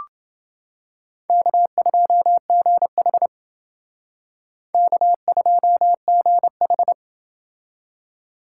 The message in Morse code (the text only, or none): K2GH K2GH